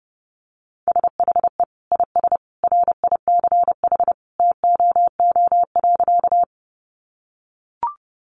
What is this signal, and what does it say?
Morse code, heard as a keyed tone, about 30 wpm, 700 Hz. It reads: SHE IS RICH TOO.
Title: Sentences from Top 1100 Words 30wpm